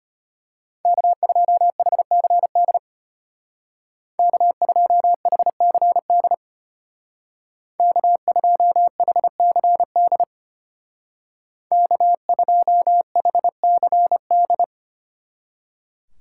Morse code, heard as a keyed tone, about 33 wpm, 700 Hz. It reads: K2HCD K2HCD K2HCD K2HCD